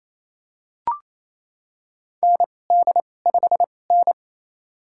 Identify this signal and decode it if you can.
Morse code, heard as a keyed tone, about 28 wpm, spaced as for 20 wpm, 700 Hz. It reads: ND5N
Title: USA Call Signs - 4 Characters - 2x Character Spacing - 28wpm